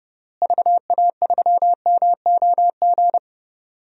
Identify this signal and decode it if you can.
Morse code, heard as a keyed tone, about 30 wpm, 700 Hz. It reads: VA3MOG
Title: DXCC-100 Call Signs - 6 Characters - Repeated 2x - 30wpm